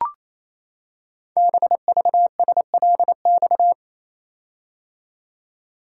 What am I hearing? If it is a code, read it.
Morse code, heard as a keyed tone, about 29 wpm, 700 Hz. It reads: BVSLX
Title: Code Groups - 5 Letter:Numbers - 28wpm